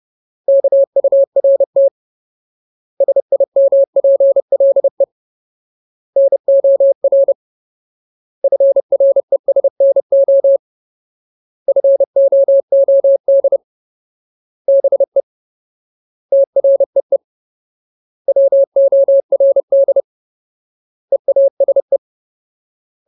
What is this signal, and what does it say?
Morse code, heard as a keyed tone, about 30 wpm, 550 Hz. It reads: KURT SIMPLE NOR FRESNO FOOD BE TREE WORD EASE
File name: Ultimate Word List / 30WPM Ultimate